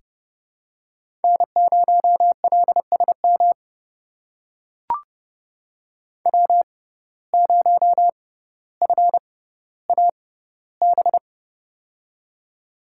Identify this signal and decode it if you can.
Morse code, heard as a keyed tone, about 30 wpm, spaced as for 15 wpm, 700 Hz. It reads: N0LSM W0FAB
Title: USA Call Signs - 5 Characters - 6x Character Spacing - 30wpm